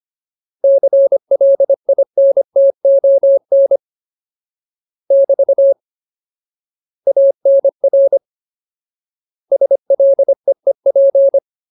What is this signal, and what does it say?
Morse code, heard as a keyed tone, about 25 wpm, 550 Hz. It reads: CLINTON <BT> ANR SLEEP
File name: Ultimate Word List / 25WPM Ultimate